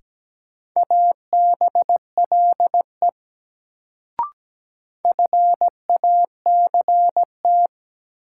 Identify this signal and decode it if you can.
Morse code, heard as a keyed tone, about 17 wpm, 700 Hz. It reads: ABLE FACT
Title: Sets of 2 Words - New Word Drill - 17wpm